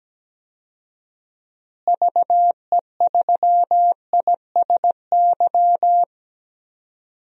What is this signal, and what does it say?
Morse code, heard as a keyed tone, about 17 wpm, 700 Hz. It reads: VE3ISY
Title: DXCC-100 Call Signs - 6 Characters - Repeated 2x - 17wpm